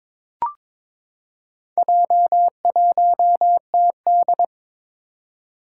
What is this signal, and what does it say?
Morse code, heard as a keyed tone, about 22 wpm, 700 Hz. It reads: J1TD
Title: Four Challenging Letter-Number Combinations 22wpm